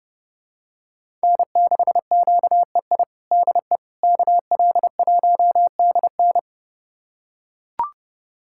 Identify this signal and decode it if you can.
Morse code, heard as a keyed tone, about 30 wpm, 700 Hz. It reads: N6QEI DE KL1DN
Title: USA Callsign DE Callsign - 5 Characters - 30wpm